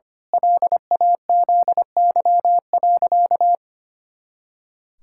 Morse code, heard as a keyed tone, about 25 wpm, 700 Hz. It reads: LAZY.
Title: Pangram Challenge v2 - 25wpm